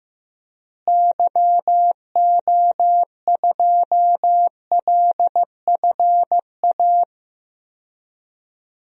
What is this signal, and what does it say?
Morse code, heard as a keyed tone, about 15 wpm, 700 Hz. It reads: YO2LFA
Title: DXCC-100 Call Signs - 6 Characters - Repeated 2x - 15wpm